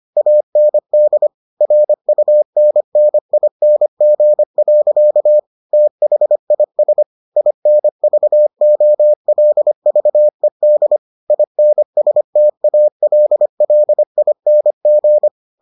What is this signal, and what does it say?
Morse code, heard as a keyed tone, about 25 wpm, 600 Hz. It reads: AND RUNNING. THIS INVOLVED INSTALLING